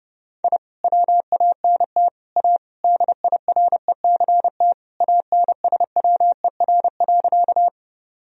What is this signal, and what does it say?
Morse code, heard as a keyed tone, about 30 wpm, 700 Hz. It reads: I WANT A DIRECT ANSWER.
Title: Sentences from Top 1000 Words - Review Part 2 - 30wpm